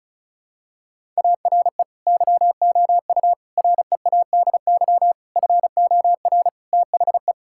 Morse code, heard as a keyed tone, about 36 wpm, 700 Hz. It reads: ARE YOU READY FOR THE
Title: Sentences from Top 1000 Words - Review Part 6 - 35wpm